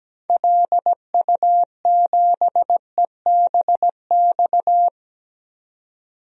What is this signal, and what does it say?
Morse code, heard as a keyed tone, about 17 wpm, 700 Hz. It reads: LU7EBX
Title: DXCC-100 Call Signs - 6 Characters - 17wpm